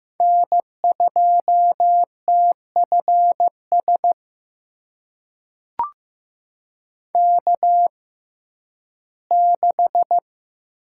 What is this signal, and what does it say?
Morse code, heard as a keyed tone, about 15 wpm, spaced as for 11 wpm, 700 Hz. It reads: N2TFS K6
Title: USA Call Signs - 6x Character Spacing - 15wpm